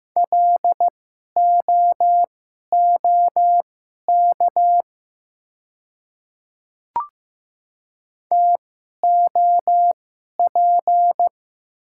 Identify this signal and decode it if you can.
Morse code, heard as a keyed tone, about 15 wpm, spaced as for 11 wpm, 700 Hz. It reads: LOOK TOP
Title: 2x Character Spacing - Top 500 Words - 15wpm